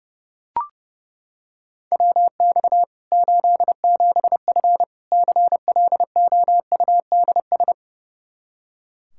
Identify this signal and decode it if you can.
Morse code, heard as a keyed tone, about 32 wpm, 700 Hz. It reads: WX 87F CLOUDS
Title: K1USN SST QSO Practice 30wpm